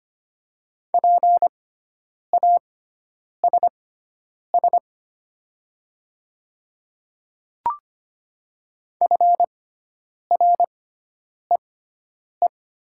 Morse code, heard as a keyed tone, about 25 wpm, spaced as for 9 wpm, 700 Hz. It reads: PASS FREE
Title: 6x Character Spacing - Top 500 Words - 25wpm